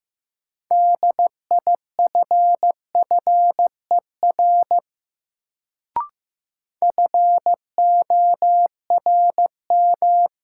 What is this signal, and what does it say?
Morse code, heard as a keyed tone, about 15 wpm, 700 Hz. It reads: DIFFER FORM
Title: Sentences from Top 1100 Words - New Word Drill Alt - 15wpm